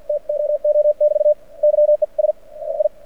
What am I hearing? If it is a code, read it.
Morse code, heard as a keyed tone, about 52 wpm, spaced as for 44 wpm, 590 Hz. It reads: TOO <BT> YES I
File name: archived / CRSnet